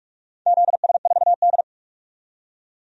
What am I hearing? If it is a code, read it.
Morse code, heard as a keyed tone, about 42 wpm, 700 Hz. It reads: ZS4D
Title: DXCC-100 Call Signs - 4 Characters - 45wpm